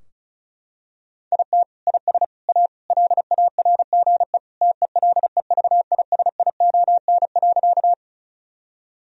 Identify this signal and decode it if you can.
Morse code, heard as a keyed tone, about 34 wpm, 700 Hz. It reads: IT IS A LARGE TELEVISION.
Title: Sentences from Top 1000 Words - Review Part 4 - 35wpm